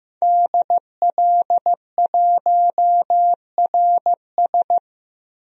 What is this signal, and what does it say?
Morse code, heard as a keyed tone, about 15 wpm, 700 Hz. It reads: DL1RS
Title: DXCC-100 Call Signs - 5 Characters - Repeated 3x - 15wpm